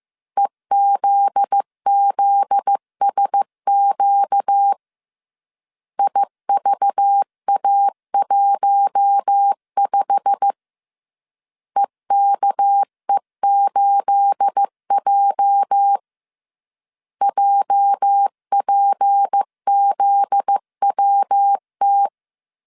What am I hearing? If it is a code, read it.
Morse code, heard as a keyed tone, about 15 wpm, 800 Hz. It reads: EZZSQ IVA15 EKE8J JPZWT
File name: CW / DET / 1 9 1500C S A 14wpm 3222